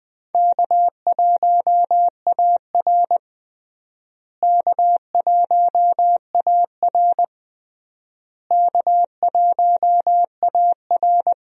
Morse code, heard as a keyed tone, about 20 wpm, 700 Hz. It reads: K1AR K1AR K1AR
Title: USA Call Signs - 4 Characters - Repeated 3x - 20wpm